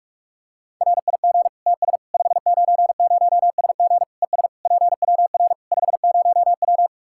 Morse code, heard as a keyed tone, about 46 wpm, 700 Hz. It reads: RIG TS 590SG ES PWR 50W
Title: K1USN SST QSO Practice 45wpm